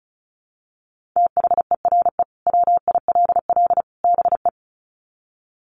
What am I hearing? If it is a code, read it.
Morse code, heard as a keyed tone, about 34 wpm, 700 Hz. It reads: THERE WILL BE
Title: Sets of 3 Words 35wpm